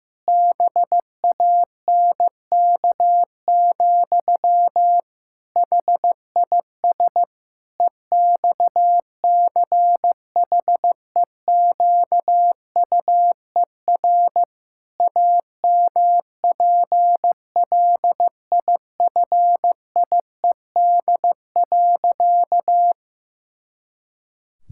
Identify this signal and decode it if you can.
Morse code, heard as a keyed tone, about 16 wpm, 700 Hz. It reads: BANK, HIS EXCHEQUER AMPLIFIED.
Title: Pangram Challenge v2 - 15wpm